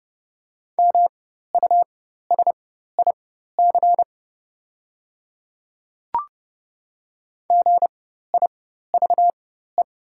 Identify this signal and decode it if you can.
Morse code, heard as a keyed tone, about 31 wpm, spaced as for 14 wpm, 700 Hz. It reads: MUSIC GIVE
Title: 4x Character Spacing - Top 300 Words - 30wpm